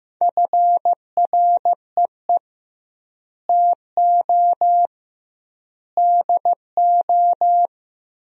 Morse code, heard as a keyed tone, about 15 wpm, 700 Hz. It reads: FREE TO DO
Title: 2x Word-Spacing - Sentences from Top 500 Words - 15wpm